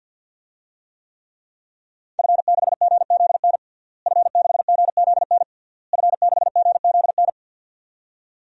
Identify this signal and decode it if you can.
Morse code, heard as a keyed tone, about 52 wpm, 700 Hz. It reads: F6GZN F6GZN F6GZN
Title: DXCC-100 Call Signs - 5 Characters - Repeated 3x - 50wpm